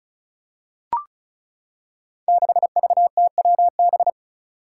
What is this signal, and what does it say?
Morse code, heard as a keyed tone, about 36 wpm, 700 Hz. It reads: 6VTWB
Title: Five Challenging Letter-Number Combinations 35wpm